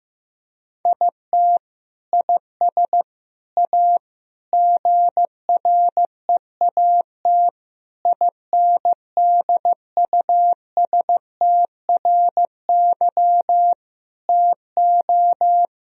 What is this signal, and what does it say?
Morse code, heard as a keyed tone, about 16 wpm, 700 Hz. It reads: IT IS A GREAT INDUSTRY TO
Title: Sentences from Top 1000 Words - Review Part 4 - 15wpm